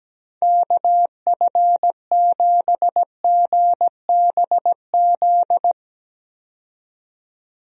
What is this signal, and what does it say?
Morse code, heard as a keyed tone, about 17 wpm, 700 Hz. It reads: KF7GBZ
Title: USA Call Signs - 6 Characters - 17wpm